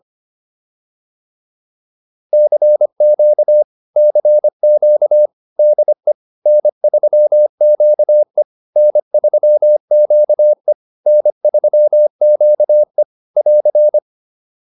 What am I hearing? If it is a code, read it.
Morse code, heard as a keyed tone, about 25 wpm, 600 Hz. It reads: CQ CQ DE N3QE N3QE N3QE <AR>
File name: QSO101 25